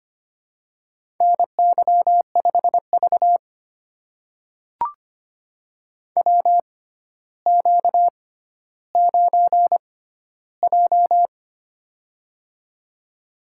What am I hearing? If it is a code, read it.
Morse code, heard as a keyed tone, about 25 wpm, spaced as for 13 wpm, 700 Hz. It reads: NY5V WQ9J
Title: USA Call Signs - 4 Characters - 6x Character Spacing - 25wpm